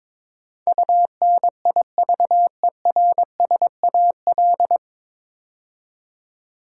Morse code, heard as a keyed tone, about 22 wpm, 700 Hz. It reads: UNIVERSAL
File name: 9 Letter Words - First 3 Letters Given - 22wpm